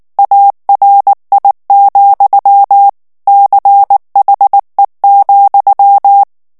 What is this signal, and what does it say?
Morse code, heard as a keyed tone, about 19 wpm, 800 Hz. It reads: ARI, CHE,
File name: morse mp3 / ITALIEN texte clair 2 vitesse 18 mots minute